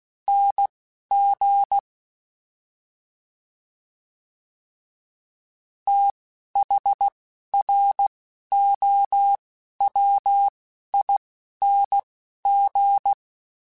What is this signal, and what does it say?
Morse code, heard as a keyed tone, about 16 wpm, 800 Hz. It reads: N G T H R O W I N G
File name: TH / TH mp3s / TH Prefixes spt @10